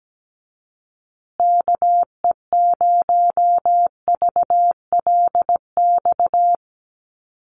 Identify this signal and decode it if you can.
Morse code, heard as a keyed tone, about 17 wpm, 700 Hz. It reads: KE0VLX
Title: USA Call Signs Repeated 3x 17wpm